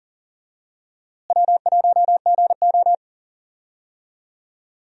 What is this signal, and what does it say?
Morse code, heard as a keyed tone, about 40 wpm, 700 Hz. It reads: W1GO
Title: USA Call Signs - 4 Characters - Repeated 3x - 40wpm